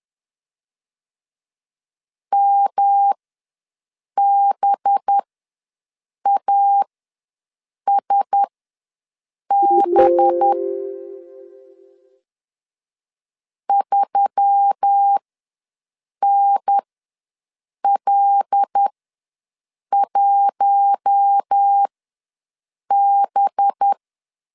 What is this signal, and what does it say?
Morse code, heard as a keyed tone, about 10 wpm, spaced as for 6 wpm, 800 Hz. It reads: MBASB 3NL1B
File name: LEZIONI / 5 LBVC / 5 LB ws 10wpm 3616